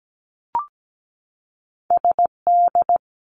Morse code, heard as a keyed tone, about 17 wpm, 700 Hz. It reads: SD
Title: US States 17wpm